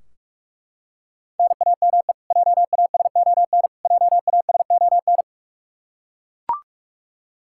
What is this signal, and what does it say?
Morse code, heard as a keyed tone, about 42 wpm, 700 Hz. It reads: NAME JASON JASON
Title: Practice QSO - Exchange 1 - Beginning - 45wpm